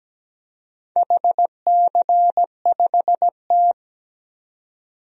Morse code, heard as a keyed tone, about 17 wpm, 700 Hz. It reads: HC5T
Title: DXCC-100 Call Signs - 4 Characters - 17wpm